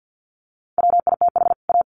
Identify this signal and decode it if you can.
Morse code, heard as a keyed tone, about 50 wpm, 700 Hz. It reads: WITH A
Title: Sets of 2 Words 50wpm